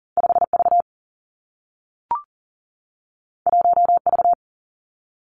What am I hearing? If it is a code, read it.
Morse code, heard as a keyed tone, about 40 wpm, 700 Hz. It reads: 5 V 1 V
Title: Two Challenging Letter-Number Combinations 40wpm